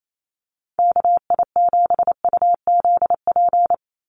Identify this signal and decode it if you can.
Morse code, heard as a keyed tone, about 28 wpm, 700 Hz. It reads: KI7UZP
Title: USA Call Signs 28wpm